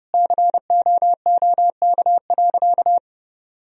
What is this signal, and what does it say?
Morse code, heard as a keyed tone, about 32 wpm, 700 Hz. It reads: COOK.
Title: Sentences from Top 1000 Words - Review Part 4 - 30wpm